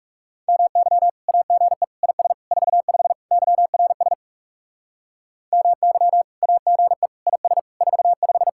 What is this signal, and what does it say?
Morse code, heard as a keyed tone, about 38 wpm, 700 Hz. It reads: MY AGE IS 45 YRS MY AGE IS 45